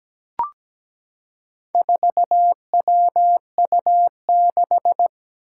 Morse code, heard as a keyed tone, about 17 wpm, 700 Hz. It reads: 4WU6
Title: Four Challenging Letter-Number Combinations 17wpm